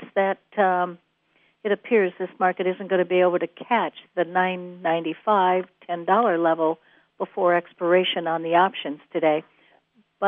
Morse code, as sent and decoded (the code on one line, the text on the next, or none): none